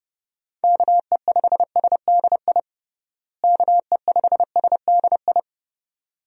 Morse code, heard as a keyed tone, about 30 wpm, 700 Hz. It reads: KE5SDI KE5SDI
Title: USA Call Signs - 6 Characters - Repeated 2x - 30wpm